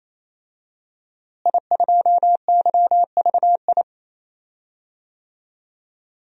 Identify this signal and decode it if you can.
Morse code, heard as a keyed tone, about 28 wpm, 700 Hz. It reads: I2YVI
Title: DXCC-100 Call Signs - 5 Characters - Repeated 3x - 28wpm